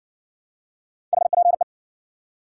Morse code, heard as a keyed tone, about 60 wpm, 700 Hz. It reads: HPE